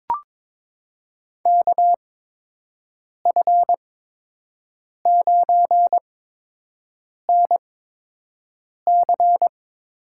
Morse code, heard as a keyed tone, about 22 wpm, spaced as for 6 wpm, 700 Hz. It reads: KF9NC
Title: USA Call Signs - 5 Characters - 8x Character Spacing - 22wpm